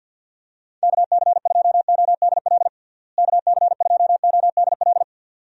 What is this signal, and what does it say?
Morse code, heard as a keyed tone, about 51 wpm, 700 Hz. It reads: KC2ODL KC2ODL
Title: USA Call Signs - 6 Characters - Repeated 2x - 50wpm